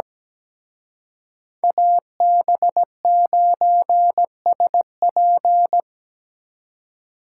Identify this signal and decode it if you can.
Morse code, heard as a keyed tone, about 19 wpm, 700 Hz. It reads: AB9SP
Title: Speed Racing - USA 2x2 Call Signs 17wpm